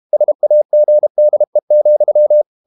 Morse code, heard as a keyed tone, about 32 wpm, 600 Hz. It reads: SAGDE,
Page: Ferie 32 wpm